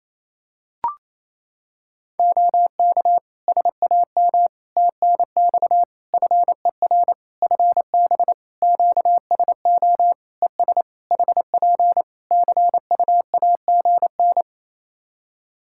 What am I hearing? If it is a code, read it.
Morse code, heard as a keyed tone, about 28 wpm, 700 Hz. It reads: OK SAM TNX FER FB QSO ES HP CUAGN
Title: Practice QSO - Exchange 4 - Ending - 28wpm